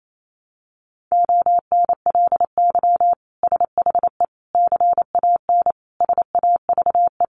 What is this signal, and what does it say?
Morse code, heard as a keyed tone, about 28 wpm, 700 Hz. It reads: ONLY SHE CAN SAVE
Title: Sentences from Top 1100 Words 28wpm